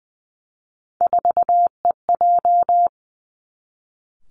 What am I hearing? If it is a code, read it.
Morse code, heard as a keyed tone, about 20 wpm, 700 Hz. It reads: 4EJ
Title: Three Challenging Letter-Number Combinations 20wpm